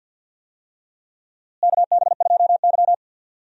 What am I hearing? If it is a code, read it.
Morse code, heard as a keyed tone, about 46 wpm, 700 Hz. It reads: KB2Y
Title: USA Call Signs - 4 Characters - Repeated 3x - 50wpm